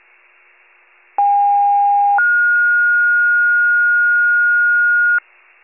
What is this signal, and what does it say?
Two-tone page, heard as a one-second tone, then a three-second tone, about 810 then 1430 Hz, with noise. It is heard from 1.2 to 5.2 s.
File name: pager / 29 FIRE / wcalert